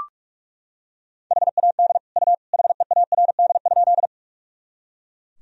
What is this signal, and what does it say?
Morse code, heard as a keyed tone, about 41 wpm, 700 Hz. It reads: HAD U HEARD?
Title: Sentences from Top 400 Words and Abbreviations 45wpm